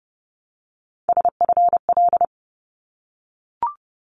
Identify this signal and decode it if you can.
Morse code, heard as a keyed tone, about 30 wpm, 700 Hz. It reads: SFL
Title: ARRL Sections 30wpm